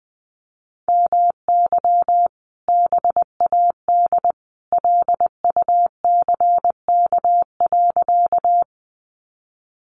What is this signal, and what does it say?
Morse code, heard as a keyed tone, about 20 wpm, 700 Hz. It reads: MY BAD LUCK.
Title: Sentences from Top 1500 Words 20wpm